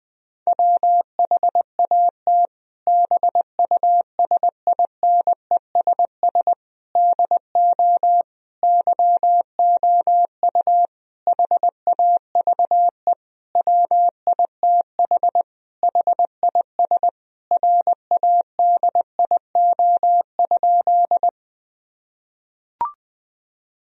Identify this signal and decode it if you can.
Morse code, heard as a keyed tone, about 20 wpm, 700 Hz. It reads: WHAT BUSINESS DO YOU HAVE WITH HIS RADIO?
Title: Sentences from Top 1000 Words - Review Part 3 - 20wpm